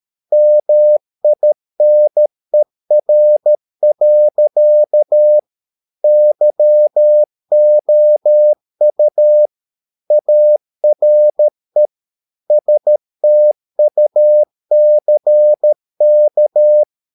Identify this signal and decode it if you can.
Morse code, heard as a keyed tone, about 13 wpm, 600 Hz. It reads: MINER. YOU ARE STUCK